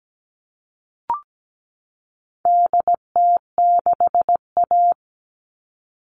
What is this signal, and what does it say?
Morse code, heard as a keyed tone, about 17 wpm, 700 Hz. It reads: DT6A
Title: All Cut Numbers - 4-Digit 17wpm